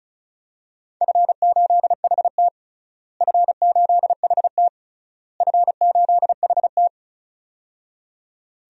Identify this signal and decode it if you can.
Morse code, heard as a keyed tone, about 36 wpm, 700 Hz. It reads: F8HT F8HT F8HT
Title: DXCC-100 Call Signs - 4 Characters - Repeated 3x - 35wpm